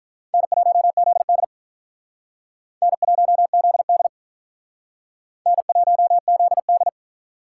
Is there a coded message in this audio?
Yes: Morse code from 0.3 to 6.9 s, heard as a keyed tone, about 45 wpm, 700 Hz. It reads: N1ZD N1ZD N1ZD